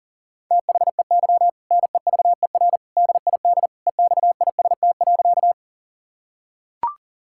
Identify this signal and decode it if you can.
Morse code, heard as a keyed tone, about 40 wpm, 700 Hz. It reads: THEY NEVER DID EXIST.
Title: Sentences from Top 1000 Words - Review Part 5 - 40wpm